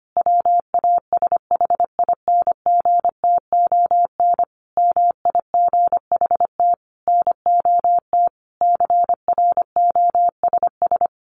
Morse code, heard as a keyed tone, about 25 wpm, 700 Hz. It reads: WASHINGTON MIGHT NOT CROSS
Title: A Sound Of Thunder 25wpm